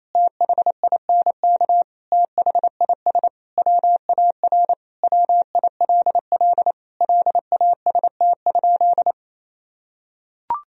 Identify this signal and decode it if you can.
Morse code, heard as a keyed tone, about 28 wpm, 700 Hz. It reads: THINK THIS WAR WILL LAST?
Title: Sentences from Top 1000 Words - Review Part 2 - 28wpm